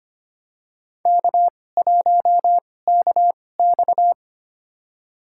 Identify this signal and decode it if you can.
Morse code, heard as a keyed tone, about 25 wpm, spaced as for 18 wpm, 700 Hz. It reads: K1KX
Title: USA Call Signs - 4 Characters - 2x Character Spacing - 25wpm